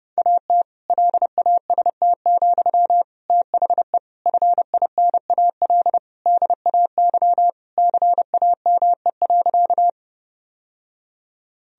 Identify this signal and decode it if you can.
Morse code, heard as a keyed tone, about 30 wpm, 700 Hz. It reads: AT LAST, THE FINAL DAY CAME.
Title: Sentences from Top 1000 Words - Review Part 2 - 30wpm